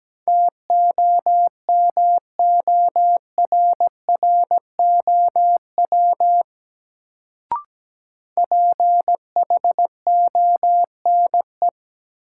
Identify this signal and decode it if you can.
Morse code, heard as a keyed tone, about 17 wpm, 700 Hz. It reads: TOMORROW PHONE
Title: Sentences from Top 1100 Words - New Word Drill - 17wpm